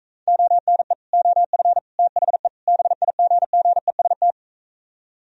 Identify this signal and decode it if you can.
Morse code, heard as a keyed tone, about 45 wpm, 700 Hz. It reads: ONE OF THE BIGGEST